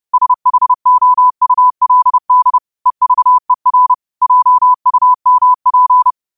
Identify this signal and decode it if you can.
Morse code, heard as a keyed tone, about 30 wpm, 1000 Hz. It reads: SHOULD EVER JUMP
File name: qotdfiles / 30wpm